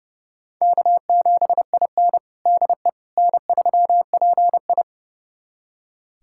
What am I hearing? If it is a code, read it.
Morse code, heard as a keyed tone, about 30 wpm, 700 Hz. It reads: K7IN DE N3PI
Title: USA Callsign DE Callsign - 4 Characters - 30wpm